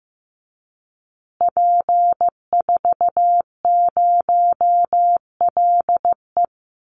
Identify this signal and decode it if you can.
Morse code, heard as a keyed tone, about 15 wpm, 700 Hz. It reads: P40LE
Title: Call Signs Repeated 2x - 15wpm